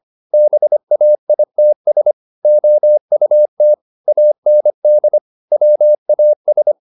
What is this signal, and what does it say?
Morse code, heard as a keyed tone, about 25 wpm, 600 Hz. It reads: BAITS OUT AND WAS
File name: the old man and the sea part 1 25wpm